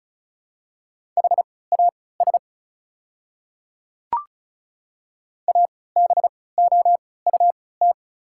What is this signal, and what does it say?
Morse code, heard as a keyed tone, about 36 wpm, spaced as for 20 wpm, 690 Hz. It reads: HAS ABOUT
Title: 3x Character Spacing - Top 100 Words - 35wpm